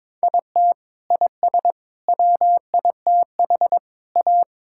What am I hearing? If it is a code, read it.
Morse code, heard as a keyed tone, about 22 wpm, 700 Hz. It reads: IT IS WITH A
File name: practice / 22wpm / Sets of 8 Words - 22wpm